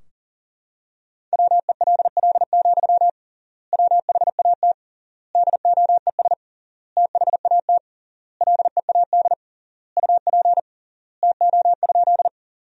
Morse code, heard as a keyed tone, about 40 wpm, 700 Hz. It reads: WELL, WHAT DOES THAT LEAD UP TO?
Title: 3x Word-Spacing - Sentences from Top 500 Words - 40wpm